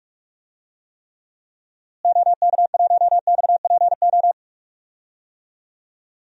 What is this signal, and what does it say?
Morse code, heard as a keyed tone, about 41 wpm, 700 Hz. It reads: OK1XPO
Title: DXCC-100 Call Signs - 6 Characters - Repeated 2x - 45wpm